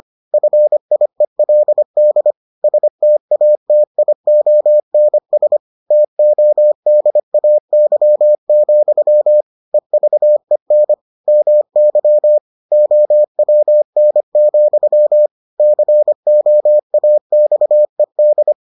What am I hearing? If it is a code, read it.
Morse code, heard as a keyed tone, about 25 wpm, 600 Hz. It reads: FIELD STATIONS TODAY, EVEN MY OWN, COAXED